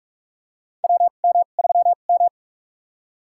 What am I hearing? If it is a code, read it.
Morse code, heard as a keyed tone, about 46 wpm, spaced as for 32 wpm, 700 Hz. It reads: WM3M
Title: USA Call Signs - 2x Character Spacing - 45wpm